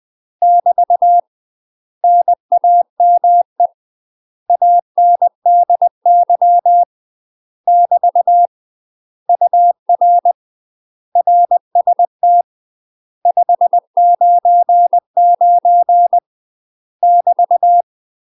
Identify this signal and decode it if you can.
Morse code, heard as a keyed tone, about 20 wpm, 700 Hz. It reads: <BT> NAME ANDY <BT> UR RST 599 <BT>
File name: UK MASTER QSOs TRAINING / 20WPM / UK MASTER QSOs 0041